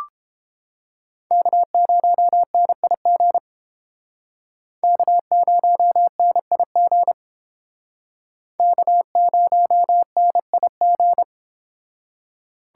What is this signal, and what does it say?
Morse code, heard as a keyed tone, about 29 wpm, 700 Hz. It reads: K0NIG K0NIG K0NIG